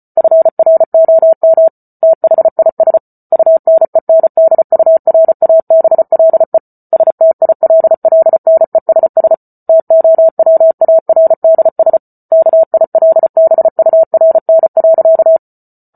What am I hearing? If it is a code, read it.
Morse code, heard as a keyed tone, about 34 wpm, 650 Hz. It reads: FROM THIS UNENDURABLE STILLNESS TOWARDS KILBURN.